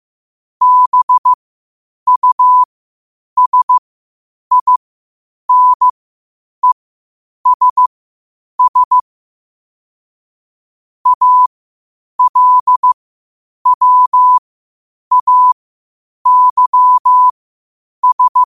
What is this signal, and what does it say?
Morse code, heard as a keyed tone, about 15 wpm, spaced as for 8 wpm, 1000 Hz. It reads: BUSINESS ALWAYS
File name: qotdfiles / 8wpm